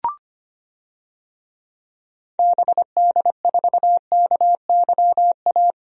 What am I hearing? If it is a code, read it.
Morse code, heard as a keyed tone, about 25 wpm, 700 Hz. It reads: BD4KYA
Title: DXCC-100 Call Signs - 6 Characters - Repeated 2x - 25wpm